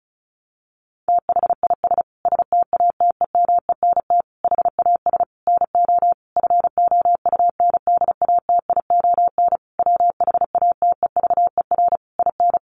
Morse code, heard as a keyed tone, about 34 wpm, 700 Hz. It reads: THIS STATEMENT HAS NO FOUNDATION WHATEVER IN
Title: Sentences from Top 1200 Words 35wpm